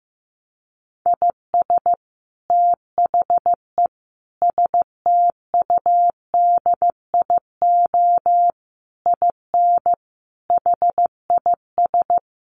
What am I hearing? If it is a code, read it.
Morse code, heard as a keyed tone, about 15 wpm, 700 Hz. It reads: IS THE STUDIO IN HIS
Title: Sentences from Top 1200 Words 15wpm